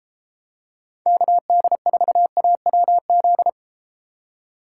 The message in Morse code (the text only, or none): KD4AWZ